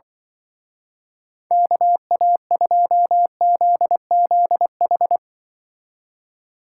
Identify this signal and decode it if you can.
Morse code, heard as a keyed tone, about 24 wpm, 700 Hz. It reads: KA2ZZH
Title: Speed Racing - USA 2x3 Call Signs 20wpm